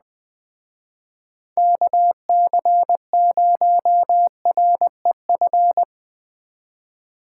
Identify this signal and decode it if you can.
Morse code, heard as a keyed tone, about 20 wpm, 700 Hz. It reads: KC0REF
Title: Speed Racing - USA 2x3 Call Signs 17wpm